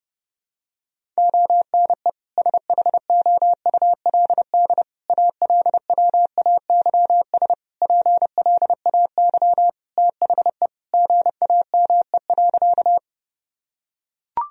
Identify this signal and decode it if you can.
Morse code, heard as a keyed tone, about 30 wpm, 700 Hz. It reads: ONE SHOULD ALWAYS PLAY THE GAME.
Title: Sentences from Top 1000 Words - Review Part 3 - 30wpm